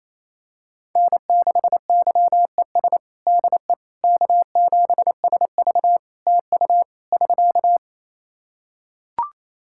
Morse code, heard as a keyed tone, about 27 wpm, 700 Hz. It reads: N6YES DE K7SV TU <SK>
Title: K1USN SST QSO Practice 28wpm